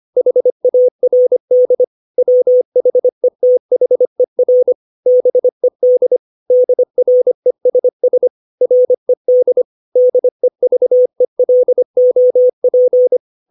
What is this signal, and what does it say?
Morse code, heard as a keyed tone, about 25 wpm, 500 Hz. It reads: HARD WHETHER BED DRESS RED DEVELOP
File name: 632 words / 632 Words 25WPM